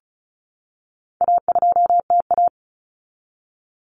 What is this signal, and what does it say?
Morse code, heard as a keyed tone, about 34 wpm, 700 Hz. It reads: A2TA
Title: Common Cut Numbers - 4-Digit 35wpm